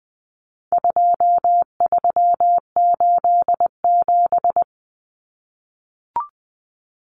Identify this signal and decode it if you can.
Morse code, heard as a keyed tone, about 20 wpm, 700 Hz. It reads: 2387
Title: Common Cut Numbers - 4-Digit 20wpm